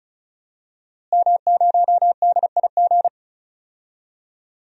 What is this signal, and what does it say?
Morse code, heard as a keyed tone, about 36 wpm, 700 Hz. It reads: M0DIG
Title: DXCC-100 Call Signs - 5 Characters - Repeated 3x - 35wpm